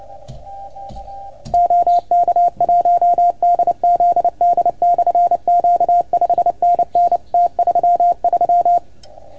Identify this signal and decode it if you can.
Morse code, heard as a keyed tone, about 30 wpm, 690 Hz. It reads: OK1DZD/Q 5NN T33
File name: Archiv / CUC310 / OK1FGD CUC 310 pileup